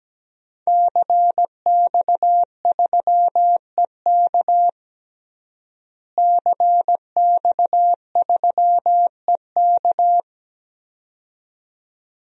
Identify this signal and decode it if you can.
Morse code, heard as a keyed tone, about 17 wpm, 700 Hz. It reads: CX3EK CX3EK
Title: DXCC-100 Call Signs - 5 Characters - Repeated 2x - 17wpm